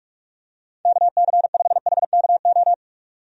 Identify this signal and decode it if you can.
Morse code, heard as a keyed tone, about 46 wpm, 700 Hz. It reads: KC5HKO
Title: USA Call Signs - 6 Characters - 45wpm